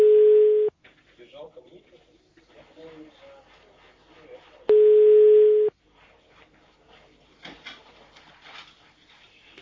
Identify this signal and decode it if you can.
Ringback tone (Europe, 425 Hz), with speech.
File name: 1022. .8e80545f-fb06-45dd-b9a5